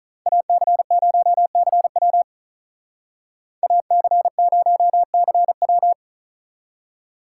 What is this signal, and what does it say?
Morse code, heard as a keyed tone, about 45 wpm, 700 Hz. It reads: AC0CW AC0CW